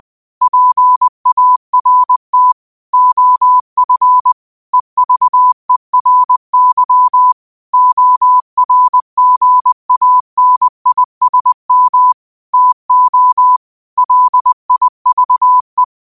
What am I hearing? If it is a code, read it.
Morse code, heard as a keyed tone, about 20 wpm, 1000 Hz. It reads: PART OF EVERY ORGANISM TO LIVE
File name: qotdfiles / 20wpm